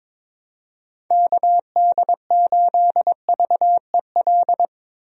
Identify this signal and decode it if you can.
Morse code, heard as a keyed tone, about 22 wpm, 700 Hz. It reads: KD8VEL
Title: USA Call Signs - 6 Characters - Repeated 3x - 22wpm